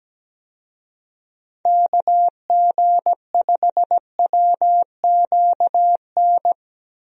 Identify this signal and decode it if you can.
Morse code, heard as a keyed tone, about 17 wpm, 700 Hz. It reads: KG5WQN
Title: USA Call Signs - 6 Characters - Repeated 3x - 17wpm